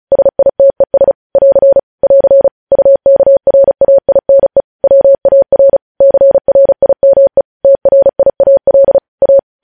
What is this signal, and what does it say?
Morse code, heard as a keyed tone, about 36 wpm, 550 Hz. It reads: SITES <AR> <AR> UKRAINE WAR CRIME TRIAL A
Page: news headlines in morse code at 35 wpm